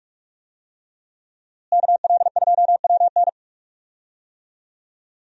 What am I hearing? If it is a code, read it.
Morse code, heard as a keyed tone, about 46 wpm, 700 Hz. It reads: KL2WN
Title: USA Call Signs - 5 Characters - Repeated 3x - 45wpm